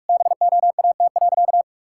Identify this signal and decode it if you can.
Morse code, heard as a keyed tone, about 41 wpm, 700 Hz. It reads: BOAT.
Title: Sentences from Top 1000 Words - Review Part 5 - 45wpm